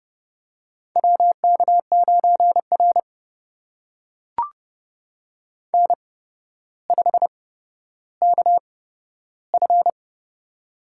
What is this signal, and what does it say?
Morse code, heard as a keyed tone, about 30 wpm, spaced as for 13 wpm, 700 Hz. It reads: WK9R N5KF
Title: USA Call Signs - 4 Characters - 8x Character Spacing - 30wpm